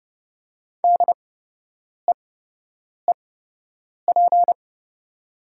Morse code, heard as a keyed tone, about 30 wpm, spaced as for 8 wpm, 700 Hz. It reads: DEEP
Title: 8x Character Spacing - Top 500 Words - 30wpm